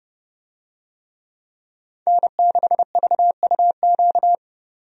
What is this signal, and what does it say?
Morse code, heard as a keyed tone, about 32 wpm, 700 Hz. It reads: N6VUQ
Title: USA Call Signs - 5 Characters - 30wpm